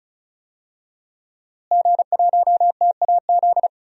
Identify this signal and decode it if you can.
Morse code, heard as a keyed tone, about 35 wpm, 700 Hz. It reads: G1TAZ
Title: DXCC-100 Call Signs - 5 Characters - Repeated 2x - 35wpm